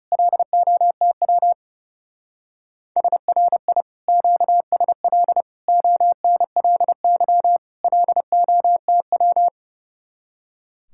Morse code, heard as a keyed tone, about 35 wpm, 700 Hz. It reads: LOTW SRI QSL ONLY LOTW
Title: Speed-Racing Common QSO Phrases Encore - 30wpm